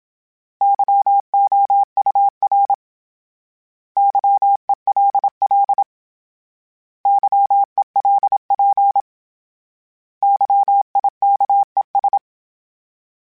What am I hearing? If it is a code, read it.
Morse code, heard as a keyed tone, about 27 wpm, 800 Hz. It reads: YOUR YELL YELP YIKES
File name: Misc / Misc-mp3s / Misc @25